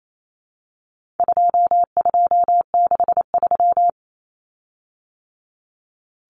Morse code, heard as a keyed tone, about 28 wpm, 700 Hz. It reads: 2263
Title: Common Cut Numbers - 4-Digit 28wpm